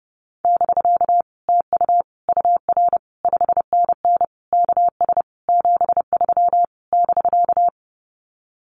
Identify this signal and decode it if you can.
Morse code, heard as a keyed tone, about 30 wpm, 700 Hz. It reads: <BK> TU UR 5NN KS 73 <BK>
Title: Common QSO Phrases Encore 30wpm